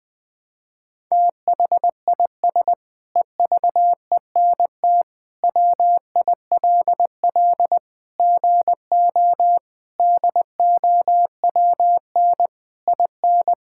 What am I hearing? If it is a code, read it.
Morse code, heard as a keyed tone, about 20 wpm, 700 Hz. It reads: THIS EVENT WILL GO DOWN IN
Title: Sentences with 9-Word Phrases - 20wpm